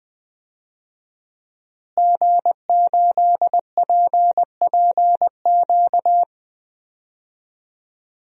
Morse code, heard as a keyed tone, about 20 wpm, 700 Hz. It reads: G8PPQ
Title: DXCC-100 Call Signs - 5 Characters - Repeated 3x - 20wpm